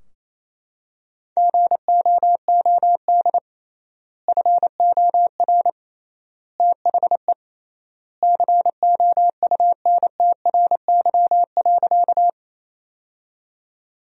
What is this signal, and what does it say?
Morse code, heard as a keyed tone, about 28 wpm, 700 Hz. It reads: GOOD FOR THE COUNTRY.
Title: 3x Word-Spacing - Sentences from Top 300 Words - 28wpm